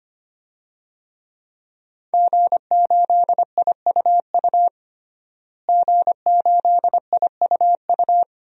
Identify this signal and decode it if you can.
Morse code, heard as a keyed tone, about 25 wpm, 700 Hz. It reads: G8IUU G8IUU
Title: DXCC-100 Call Signs - 5 Characters - Repeated 2x - 25wpm